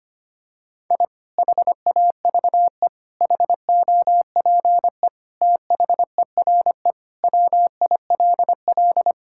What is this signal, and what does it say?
Morse code, heard as a keyed tone, about 25 wpm, 700 Hz. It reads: I HAVE HOPE THERE WILL
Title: Sentences from Top 1000 Words - Review Part 5 - 25wpm